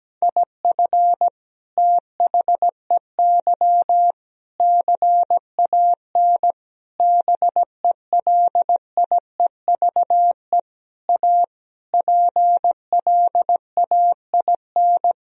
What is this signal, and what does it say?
Morse code, heard as a keyed tone, about 17 wpm, 700 Hz. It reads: IF THEY CAN BELIEVE A PLAIN
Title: Sentences from Top 1000 Words - Review Part 2 - 17wpm